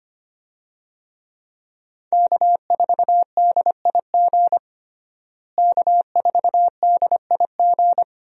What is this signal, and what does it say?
Morse code, heard as a keyed tone, about 25 wpm, 700 Hz. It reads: K4DIG K4DIG
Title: USA Call Signs - 5 Characters - Repeated 2x - 25wpm